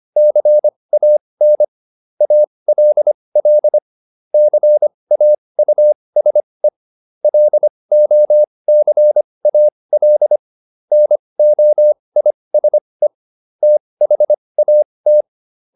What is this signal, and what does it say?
Morse code, heard as a keyed tone, about 25 wpm, spaced as for 20 wpm, 600 Hz. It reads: CAN ALL CAUSE LOCAL NOISE THAT